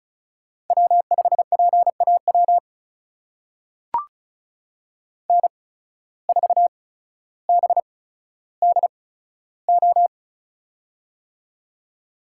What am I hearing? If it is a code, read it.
Morse code, heard as a keyed tone, about 36 wpm, spaced as for 15 wpm, 700 Hz. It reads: W5PAW N4BDO
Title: USA Call Signs - 5 Characters - 8x Character Spacing - 35wpm